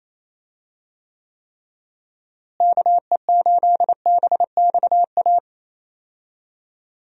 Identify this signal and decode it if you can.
Morse code, heard as a keyed tone, about 28 wpm, 700 Hz. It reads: KE8BXA
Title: USA Call Signs - 6 Characters - Repeated 3x - 28wpm